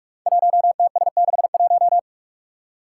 Morse code, heard as a keyed tone, about 44 wpm, 700 Hz. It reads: 1TS61
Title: Five Challenging Letter-Number Combinations 45wpm